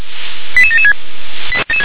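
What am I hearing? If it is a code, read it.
Five-tone selective call (ZVEI), heard as quick runs of five tones, tones of about 70 ms. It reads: 88776
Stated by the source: ZVEI five-tone call 88799 received by SDR, channel 357.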